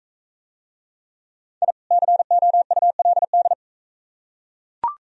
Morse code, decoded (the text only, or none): I COULD